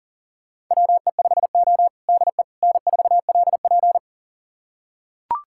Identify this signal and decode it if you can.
Morse code, heard as a keyed tone, about 40 wpm, 700 Hz. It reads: WE5O DE N4LP
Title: USA Callsign DE Callsign - 4 Characters - 40wpm